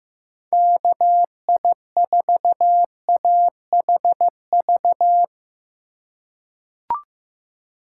Morse code, heard as a keyed tone, about 15 wpm, 700 Hz. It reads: KI4AHV
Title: USA Call Signs - 6 Characters - 3x Character Spacing - 15wpm